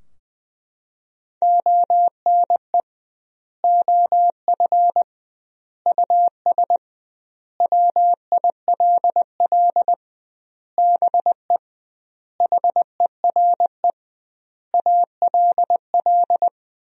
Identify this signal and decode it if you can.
Morse code, heard as a keyed tone, about 21 wpm, 700 Hz. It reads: ONE OF US WILL BE HERE ALL
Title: 2x Word-Spacing - Sentences from Top 300 Words - 20wpm